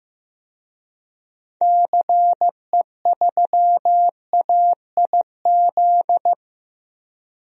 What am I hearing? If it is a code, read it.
Morse code, heard as a keyed tone, about 15 wpm, 700 Hz. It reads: CE3AIZ
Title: DXCC-100 Call Signs - 6 Characters - 15wpm